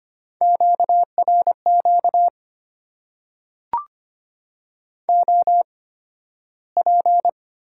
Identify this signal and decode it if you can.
Morse code, heard as a keyed tone, about 25 wpm, spaced as for 13 wpm, 700 Hz. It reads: QRQ OP
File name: Top 90 QSO Elements - 8x Character Spacing - 25wpm